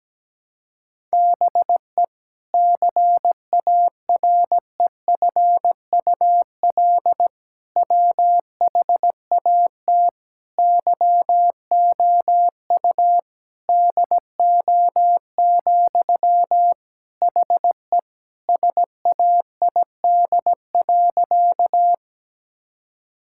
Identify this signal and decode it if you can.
Morse code, heard as a keyed tone, about 17 wpm, 700 Hz. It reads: BE CAREFUL WHAT YOU DO, HE SAID.
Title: Sentences from Top 1600 Words 17wpm